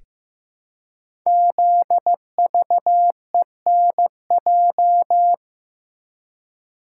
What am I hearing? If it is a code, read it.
Morse code, heard as a keyed tone, about 15 wpm, 700 Hz. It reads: ZVENJ
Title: Code Groups - 5 Letters - 15wpm